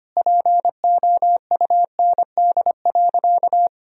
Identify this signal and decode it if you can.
Morse code, heard as a keyed tone, about 25 wpm, 700 Hz. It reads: POUND.
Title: Sentences from Top 1000 Words - Review Part 2 - 25wpm